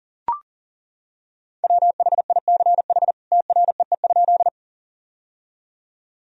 Morse code, heard as a keyed tone, about 40 wpm, 700 Hz. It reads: WHICH TREE?
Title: Sentences from Top 1000 Words - Review Part 1 - 40wpm